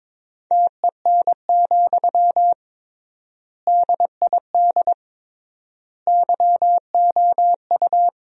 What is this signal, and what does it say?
Morse code, heard as a keyed tone, about 22 wpm, 700 Hz. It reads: TEN, DID YOU
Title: 3x Word-Spacing - Sentences from Top 400 Words - 22wpm